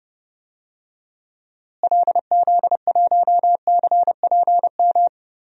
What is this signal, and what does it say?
Morse code, heard as a keyed tone, about 30 wpm, 700 Hz. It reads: LZ1CPM
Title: DXCC-100 Call Signs - 6 Characters - Repeated 3x - 30wpm